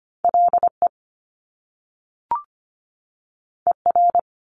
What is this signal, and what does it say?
Morse code, heard as a keyed tone, about 25 wpm, 700 Hz. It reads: LE ER
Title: Top Two Letter Combinations in Words 25wpm